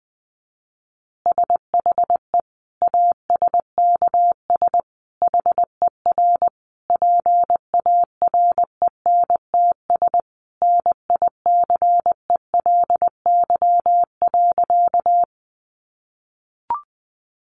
Morse code, heard as a keyed tone, about 20 wpm, 700 Hz. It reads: SHE ASKS HER PARENTS NICELY.
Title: Easy Phrases 20wpm